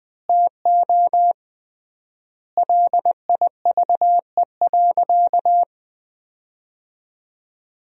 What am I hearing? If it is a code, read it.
Morse code, heard as a keyed tone, about 21 wpm, 700 Hz. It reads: TO LIVE.
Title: 3x Word-Spacing - Sentences from Top 300 Words - 20wpm